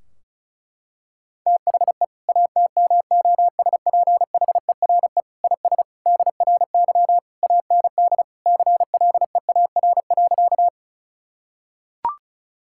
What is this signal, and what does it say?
Morse code, heard as a keyed tone, about 34 wpm, 700 Hz. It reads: THE ATMOSPHERE IS DRY AND CLEAR.
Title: Sentences from Top 1600 Words 35wpm